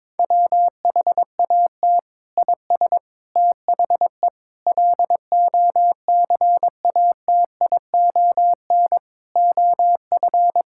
Morse code, heard as a keyed tone, about 22 wpm, 700 Hz. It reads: WHAT IS THE LOCATION OF
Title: Sentences from Top 1000 Words - Review Part 6 - 22wpm